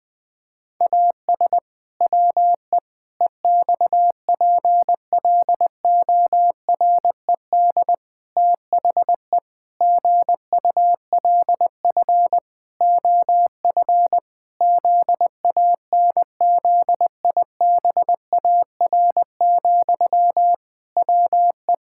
Morse code, heard as a keyed tone, about 20 wpm, 700 Hz. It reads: AS WE EXPLORED THE GULF OF ZANZIBAR, WE
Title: Pangram Challenge v2 - 20wpm